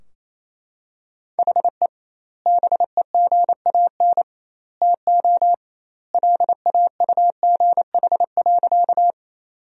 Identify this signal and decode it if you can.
Morse code, heard as a keyed tone, about 28 wpm, 700 Hz. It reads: HE BEGAN TO LAUGH.
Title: 2x Word-Spacing - Sentences from Top 500 Words - 28wpm